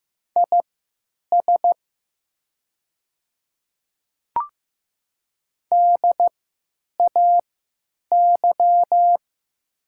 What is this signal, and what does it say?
Morse code, heard as a keyed tone, about 15 wpm, spaced as for 9 wpm, 700 Hz. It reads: IS DAY
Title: 3x Character Spacing - Top 100 Words - 15wpm